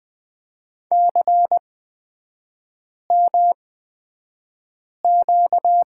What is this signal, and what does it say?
Morse code, heard as a keyed tone, about 20 wpm, spaced as for 5 wpm, 700 Hz. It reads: CMQ
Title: 243 Lesson 39e - Letter z - Three Characters 20wpm